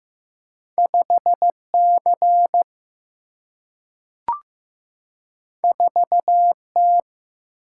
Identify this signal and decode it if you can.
Morse code, heard as a keyed tone, about 15 wpm, 700 Hz. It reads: 5 C 4 T
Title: Single-Letter Single-Number 15wpm